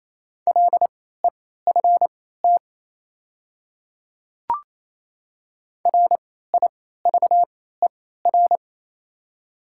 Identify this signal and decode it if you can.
Morse code, heard as a keyed tone, about 27 wpm, spaced as for 16 wpm, 700 Hz. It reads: LEFT RIVER
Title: 3x Character Spacing - Top 500 Words - 28wpm